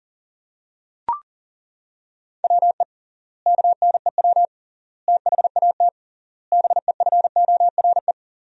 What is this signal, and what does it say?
Morse code, heard as a keyed tone, about 40 wpm, 700 Hz. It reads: WE KNEW THAT BEFORE
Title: 3x Word-Spacing - Sentences from Top 400 Words - 40wpm